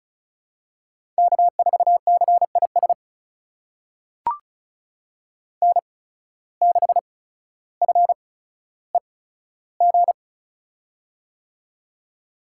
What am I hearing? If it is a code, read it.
Morse code, heard as a keyed tone, about 36 wpm, spaced as for 15 wpm, 700 Hz. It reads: K4CIS N6FEG
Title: USA Call Signs - 5 Characters - 8x Character Spacing - 35wpm